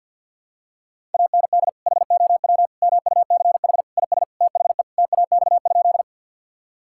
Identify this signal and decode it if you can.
Morse code, heard as a keyed tone, about 46 wpm, 690 Hz. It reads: AND HOW MUCH IS THE TAX?
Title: Sentences from Top 1000 Words - Review Part 5 - 50wpm